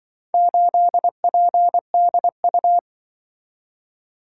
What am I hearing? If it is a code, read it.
Morse code, heard as a keyed tone, about 23 wpm, 700 Hz. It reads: 8PDU
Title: Speed-Racing - Four Letter-Number Combinations - 20wpm